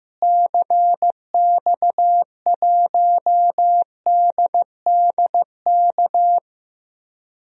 Morse code, heard as a keyed tone, about 16 wpm, 700 Hz. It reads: CX1DDK
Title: DXCC-100 Call Signs - 6 Characters - Repeated 3x - 15wpm